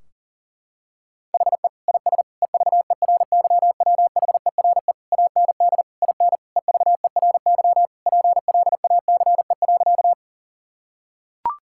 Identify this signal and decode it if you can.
Morse code, heard as a keyed tone, about 40 wpm, 700 Hz. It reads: HE IS EVERYWHERE AND IN EVERY PLACE.
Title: Sentences from Top 1600 Words 40wpm